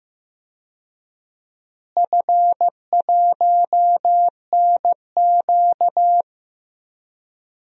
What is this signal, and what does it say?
Morse code, heard as a keyed tone, about 15 wpm, 700 Hz. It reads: F1NQ
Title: DXCC-100 Call Signs - 4 Characters - Repeated 2x - 15wpm